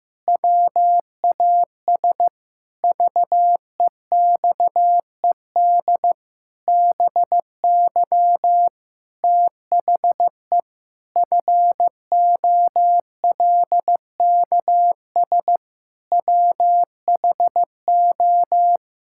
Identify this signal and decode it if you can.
Morse code, heard as a keyed tone, about 15 wpm, 700 Hz. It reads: WAS VEXED BY THE FOLKS WHO
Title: Pangram Challenge v2 - 15wpm